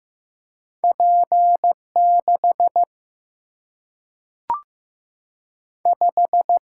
Morse code, heard as a keyed tone, about 15 wpm, 700 Hz. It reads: P 6 5
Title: Single-Letter Single-Number 15wpm